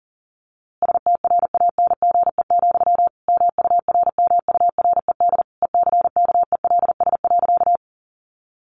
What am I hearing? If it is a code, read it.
Morse code, heard as a keyed tone, about 40 wpm, 700 Hz. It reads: STRANGE, MURMURED ECKELS.
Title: A Sound Of Thunder 40wpm